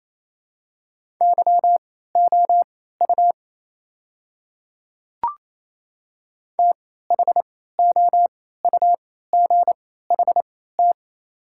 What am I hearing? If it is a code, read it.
Morse code, heard as a keyed tone, about 27 wpm, spaced as for 16 wpm, 690 Hz. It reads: YOU THOUGHT
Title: 3x Character Spacing - Top 300 Words - 28wpm